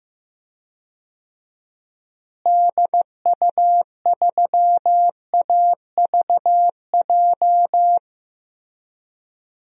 Morse code, heard as a keyed tone, about 15 wpm, 700 Hz. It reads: DU3AVJ
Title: DXCC-100 Call Signs - 6 Characters - Repeated 2x - 15wpm